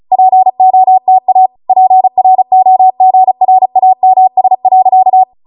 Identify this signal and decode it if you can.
Morse code, heard as a keyed tone, about 36 wpm, 750 Hz. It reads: POTA PROGRAMS.